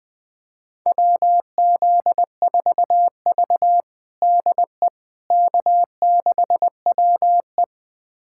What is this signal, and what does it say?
Morse code, heard as a keyed tone, about 21 wpm, 700 Hz. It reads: WZ4V DE K6WE
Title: USA Callsign DE Callsign - 4 Characters - 20wpm